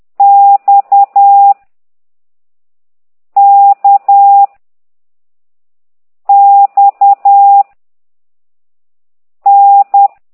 Morse code, heard as a keyed tone, about 10 wpm, 800 Hz. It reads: X K X N